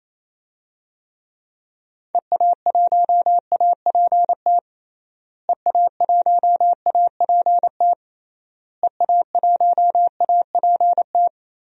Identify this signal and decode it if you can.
Morse code, heard as a keyed tone, about 27 wpm, 700 Hz. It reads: EA1APT EA1APT EA1APT
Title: DXCC-100 Call Signs - 6 Characters - Repeated 3x - 28wpm